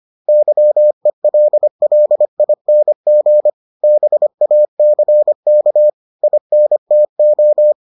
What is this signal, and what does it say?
Morse code, heard as a keyed tone, about 25 wpm, 600 Hz. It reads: YELLING BACK INTO